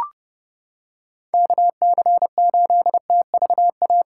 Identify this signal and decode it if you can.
Morse code, heard as a keyed tone, about 28 wpm, 700 Hz. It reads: KC8TVA